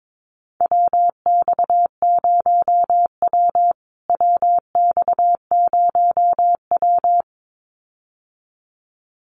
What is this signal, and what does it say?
Morse code, heard as a keyed tone, about 22 wpm, 700 Hz. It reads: WX0W WX0W
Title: USA Call Signs Repeated 2x 22wpm